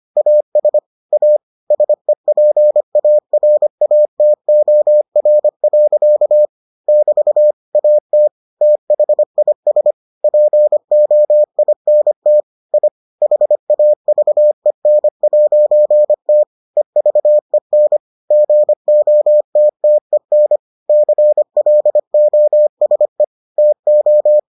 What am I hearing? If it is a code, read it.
Morse code, heard as a keyed tone, about 25 wpm, 600 Hz. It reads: AS A SEPARATOR. <BT> AT THIS POINT I HAVEN'T EVEN GOTTEN CLOSE TO